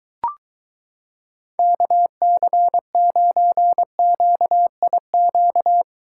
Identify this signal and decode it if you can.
Morse code, heard as a keyed tone, about 23 wpm, 700 Hz. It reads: KC9QIQ